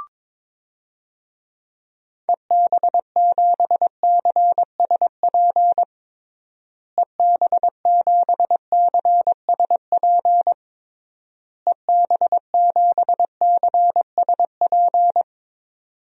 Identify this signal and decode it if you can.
Morse code, heard as a keyed tone, about 22 wpm, 700 Hz. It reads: EB7CSP EB7CSP EB7CSP
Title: DXCC-100 Call Signs - 6 Characters - Repeated 3x - 22wpm